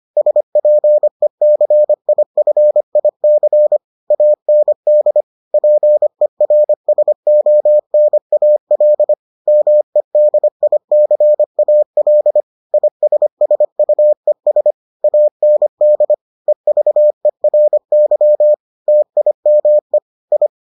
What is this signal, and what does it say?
Morse code, heard as a keyed tone, about 25 wpm, 600 Hz. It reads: SPECIFIC AND PERSONAL MEDICAL ISSUES AND EVERY TIME I